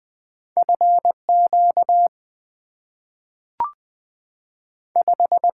Morse code, heard as a keyed tone, about 20 wpm, 700 Hz. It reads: F Q 5
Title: Two Letter-Number Combinations - 20wpm